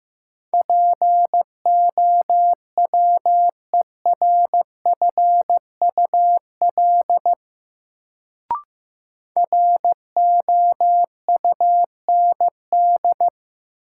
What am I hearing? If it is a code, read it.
Morse code, heard as a keyed tone, about 15 wpm, 700 Hz. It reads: POWERFUL ROUND
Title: Sets of 5 Words Encore - New Word Drill - 15wpm